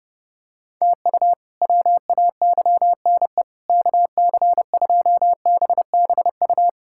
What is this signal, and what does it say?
Morse code, heard as a keyed tone, about 30 wpm, 700 Hz. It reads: TU WAYNE KC2BBU
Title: K1USN SST QSO Practice - 30wpm